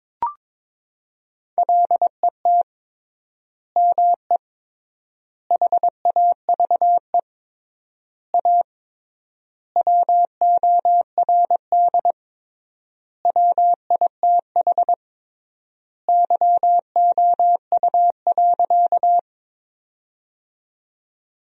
Morse code, heard as a keyed tone, about 22 wpm, 700 Hz. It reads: LET ME HAVE A WORD WITH YOU.
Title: 3x Word-Spacing - Sentences from Top 300 Words - 22wpm